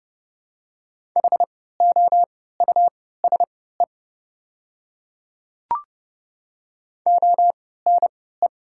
Morse code, heard as a keyed tone, about 30 wpm, spaced as for 17 wpm, 690 Hz. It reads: HOUSE ONE
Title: 3x Character Spacing - Top 200 Words - 30wpm